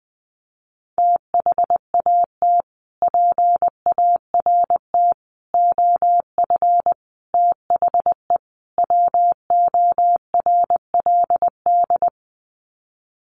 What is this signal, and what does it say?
Morse code, heard as a keyed tone, about 20 wpm, 700 Hz. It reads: THAT PART OF THE WORLD
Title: Sets of 5 Words Encore2 20wpm